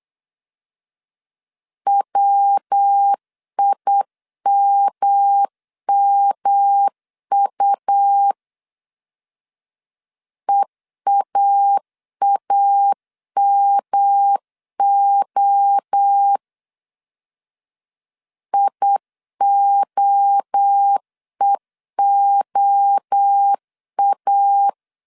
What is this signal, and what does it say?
Morse code, heard as a keyed tone, about 8 wpm, 800 Hz. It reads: WIMMU EAAMO IOEOA
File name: CW / DET / 1 3 1500C S B 8wpm 4141